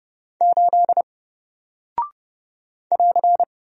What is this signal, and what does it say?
Morse code, heard as a keyed tone, about 30 wpm, 700 Hz. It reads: 8 <AR>
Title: Single Character (Letter:Number:Punctuation:Prosign) Alt 30wpm